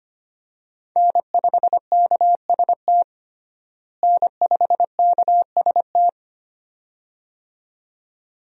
Morse code, heard as a keyed tone, about 25 wpm, 700 Hz. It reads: N5KST N5KST
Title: USA Call Signs - 5 Characters - Repeated 2x - 25wpm